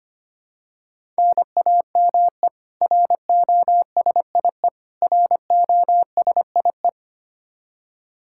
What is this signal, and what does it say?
Morse code, heard as a keyed tone, about 25 wpm, 700 Hz. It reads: NAME ROSIE ROSIE
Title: K1USN SST QSO Practice 25wpm